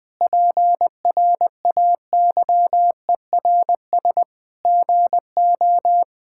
Morse code, heard as a keyed tone, about 20 wpm, 700 Hz. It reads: PRAYERS GO
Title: Sets of 8 Words - 20wpm